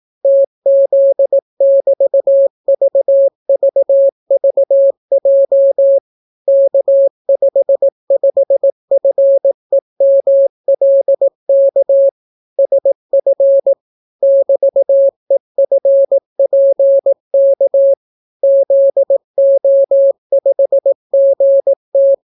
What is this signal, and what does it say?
Morse code, heard as a keyed tone, about 18 wpm, 550 Hz. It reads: TZ<BT>VVVJ K55FEMLK SF <BT>EFPK ZO5GT